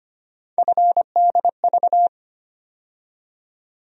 Morse code, heard as a keyed tone, about 25 wpm, 700 Hz. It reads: FDV
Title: USA Call Sign Postfixes - 3 Letters 25wpm